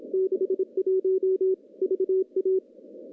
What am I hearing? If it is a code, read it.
Morse code, heard as a keyed tone, about 27 wpm, 380 Hz. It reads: 61 VA
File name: contests / CW-SS-k5tr- -0100